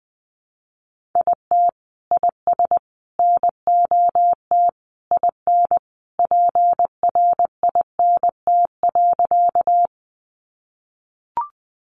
Morse code, heard as a keyed tone, about 20 wpm, 700 Hz. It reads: IT IS NOT IN PRINT.
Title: Sentences from Top 1400 Words 20wpm